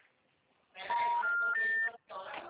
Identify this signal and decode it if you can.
Special information tone, with speech.
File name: out- -2004